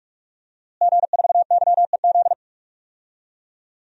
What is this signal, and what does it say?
Morse code, heard as a keyed tone, about 46 wpm, 700 Hz. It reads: G4YEZ
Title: DXCC-100 Call Signs - 5 Characters - Repeated 3x - 45wpm